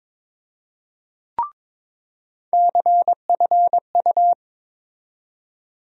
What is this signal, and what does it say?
Morse code, heard as a keyed tone, about 22 wpm, 700 Hz. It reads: CFU
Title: USA Call Sign Postfixes - 3 Letters 22wpm